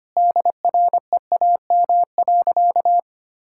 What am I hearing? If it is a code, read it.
Morse code, heard as a keyed tone, about 25 wpm, 700 Hz. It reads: DREAM.
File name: practice / 25wpm / Sentences with 4-Word Phrases - 25wpm